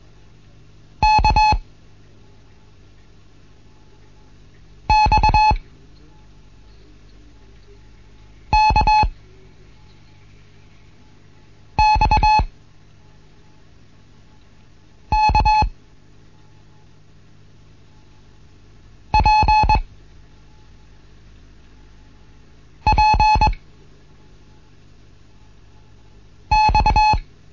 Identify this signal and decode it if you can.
Morse code, heard as a keyed tone, about 21 wpm, 860 Hz. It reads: K X K X K P P X
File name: Morsen / Kurs A 15